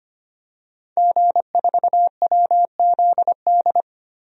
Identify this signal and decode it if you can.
Morse code, heard as a keyed tone, about 25 wpm, 700 Hz. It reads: G4WZD
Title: DXCC-100 Call Signs - 5 Characters - Repeated 2x - 25wpm